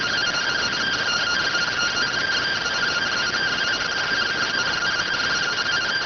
Two-tone teleprinter signal, about 170 Hz shift.